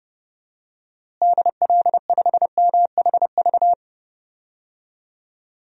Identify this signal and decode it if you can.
Morse code, heard as a keyed tone, about 32 wpm, 700 Hz. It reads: DL5MHV
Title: DXCC-100 Call Signs - 6 Characters - Repeated 2x - 30wpm